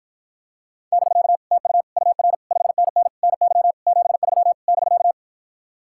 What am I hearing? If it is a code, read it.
Morse code, heard as a keyed tone, about 52 wpm, 700 Hz. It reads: <BK> TU UR 5NN NY 73 <BK>